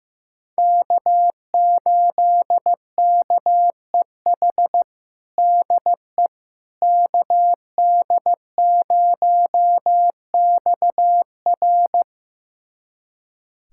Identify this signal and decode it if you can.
Morse code, heard as a keyed tone, about 15 wpm, 700 Hz. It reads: K8KEH DE KD0XR
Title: USA Callsign DE Callsign - 5 Characters - 15wpm